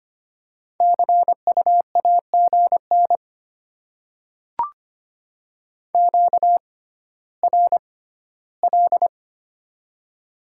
Morse code, heard as a keyed tone, about 25 wpm, spaced as for 14 wpm, 700 Hz. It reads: CU AGN QRL
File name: Top 20 QSO Elements - 6x Character Spacing - 25wpm